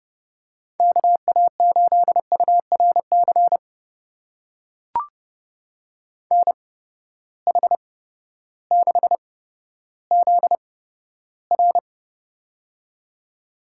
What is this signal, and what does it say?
Morse code, heard as a keyed tone, about 30 wpm, spaced as for 14 wpm, 700 Hz. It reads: KA8URC NH6ZR
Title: USA Call Signs - 8x Character Spacing - 30wpm